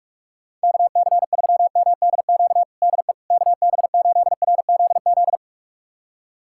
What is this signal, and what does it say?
Morse code, heard as a keyed tone, about 47 wpm, 700 Hz. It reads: KC3MDQ DE KB8RZZ
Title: USA Callsign DE Callsign 45wpm